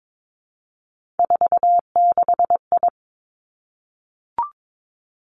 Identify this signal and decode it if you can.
Morse code, heard as a keyed tone, about 22 wpm, 700 Hz. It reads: 46I
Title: Three Challenging Letter-Number Combinations 22wpm